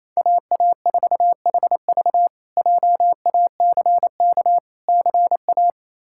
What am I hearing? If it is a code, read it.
Morse code, heard as a keyed tone, about 28 wpm, 700 Hz. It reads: AA4HV JACK CA
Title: K1USN SST QSO Practice - 28wpm